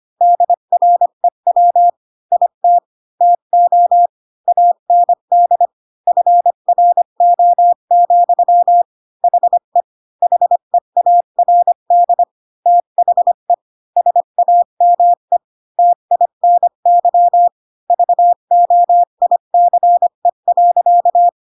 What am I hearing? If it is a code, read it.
Morse code, heard as a keyed tone, about 25 wpm, 700 Hz. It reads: DREW IT TO AND FRO, HE HEARD THE SAME TINY VOICE.
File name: Pinocchio / Pinocchio ch1 20 25 ik1hge